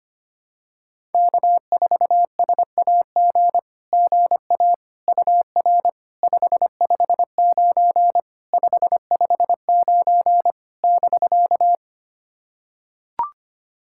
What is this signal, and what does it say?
Morse code, heard as a keyed tone, about 25 wpm, 700 Hz. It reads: K4SAG GA UR 559 559 <BK>
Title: SOTA QSO Practice 25wpm